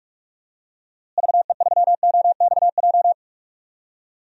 Morse code, heard as a keyed tone, about 44 wpm, 700 Hz. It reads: VE3OXJ
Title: DXCC-100 Call Signs - 6 Characters - Repeated 3x - 45wpm